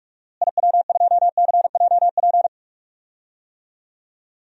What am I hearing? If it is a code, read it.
Morse code, heard as a keyed tone, about 46 wpm, 700 Hz. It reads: IW2CJP
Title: DXCC-100 Call Signs - 6 Characters - Repeated 3x - 45wpm